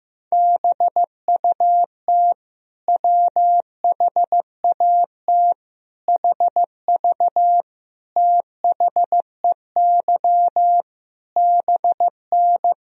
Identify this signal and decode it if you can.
Morse code, heard as a keyed tone, about 15 wpm, 700 Hz. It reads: BUT WHAT HV THEY BN
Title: Sentences from Top 100 Words and Abbreviations 15wpm